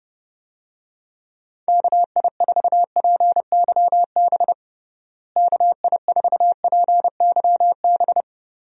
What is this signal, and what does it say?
Morse code, heard as a keyed tone, about 32 wpm, 700 Hz. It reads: KI4PYB KI4PYB
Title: USA Call Signs - 6 Characters - Repeated 3x - 30wpm